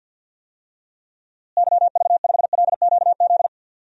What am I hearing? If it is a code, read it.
Morse code, heard as a keyed tone, about 51 wpm, 700 Hz. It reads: YV5LQZ
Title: DXCC-100 Call Signs - 6 Characters - Repeated 2x - 50wpm